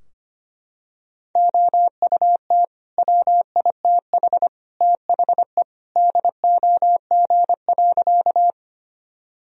Morse code, heard as a keyed tone, about 25 wpm, 700 Hz. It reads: OUT WITH THE DOG.
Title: Sentences from Top 1000 Words - Review Part 2 - 25wpm